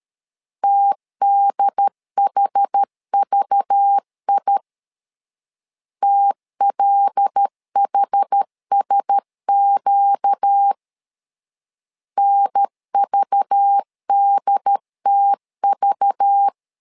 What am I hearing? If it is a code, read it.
Morse code, heard as a keyed tone, about 13 wpm, 800 Hz. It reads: TDHVI TLHSQ NVDTV
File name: LEZIONI / 5 LBVC / 5 LBV 12wpm 1855